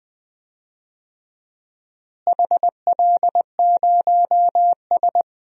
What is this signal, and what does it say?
Morse code, heard as a keyed tone, about 20 wpm, 700 Hz. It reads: HL0S
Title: DXCC-100 Call Signs - 4 Characters - Repeated 3x - 20wpm